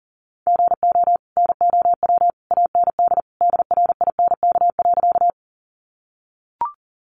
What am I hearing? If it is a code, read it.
Morse code, heard as a keyed tone, about 40 wpm, 700 Hz. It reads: GO NOW AND DRINK.
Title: Sentences from Top 1500 Words 40wpm